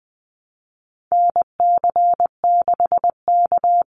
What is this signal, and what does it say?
Morse code, heard as a keyed tone, about 20 wpm, 700 Hz. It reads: NC6K
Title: Call Signs - 20wpm